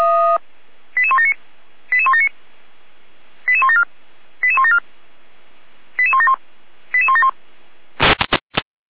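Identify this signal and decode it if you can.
Five-tone selective call (ZVEI), heard as quick runs of five tones, tones of about 70 ms. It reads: 88179 88179 88174 88174 88171 88171
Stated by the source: Five-tone ZVEI selective call 88191 received on an SDR.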